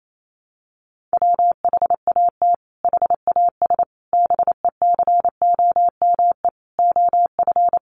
Morse code, heard as a keyed tone, about 28 wpm, 700 Hz. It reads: WHAT HAS BECOME OF
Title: Sentences from Top 600 Words 28wpm